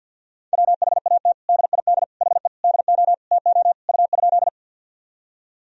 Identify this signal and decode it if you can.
Morse code, heard as a keyed tone, about 53 wpm, 700 Hz. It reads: WHAT DID HE DO TO U?
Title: Sentences from Top 100 Words and Abbreviations 50wpm